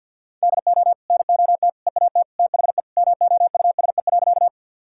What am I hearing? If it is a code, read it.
Morse code, heard as a keyed tone, about 52 wpm, 700 Hz. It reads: DO NOT EAT THE MOUSE.
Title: Sentences from Top 1000 Words - Review Part 5 - 50wpm